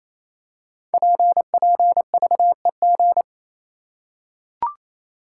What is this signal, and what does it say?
Morse code, heard as a keyed tone, about 27 wpm, 700 Hz. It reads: PPVEG
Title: Code Groups - 5 Letter:Numbers - 28wpm